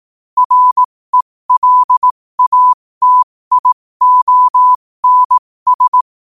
Morse code, heard as a keyed tone, about 18 wpm, spaced as for 15 wpm, 1000 Hz. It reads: RELATIONS
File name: qotdfiles / 15wpm